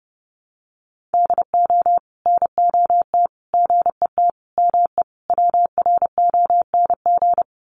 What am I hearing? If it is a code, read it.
Morse code, heard as a keyed tone, about 30 wpm, 700 Hz. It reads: DO NOT GET ME WRONG
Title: Sets of 5 Words Encore 30wpm